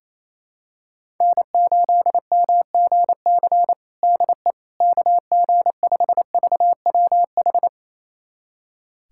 Morse code, heard as a keyed tone, about 28 wpm, 700 Hz. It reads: N8MGC DE KG5VWH
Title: USA Callsign DE Callsign 28wpm